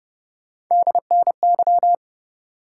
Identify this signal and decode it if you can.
Morse code, heard as a keyed tone, about 30 wpm, 700 Hz. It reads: DNY